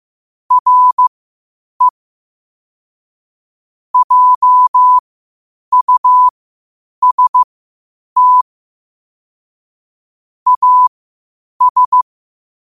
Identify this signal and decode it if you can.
Morse code, heard as a keyed tone, about 15 wpm, spaced as for 8 wpm, 1000 Hz. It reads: RE JUST AS
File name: qotdfiles / 8wpm